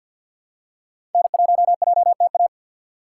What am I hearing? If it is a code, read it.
Morse code, heard as a keyed tone, about 50 wpm, 700 Hz. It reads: N1JTA
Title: USA Call Signs - 5 Characters - 3x Character Spacing - 50wpm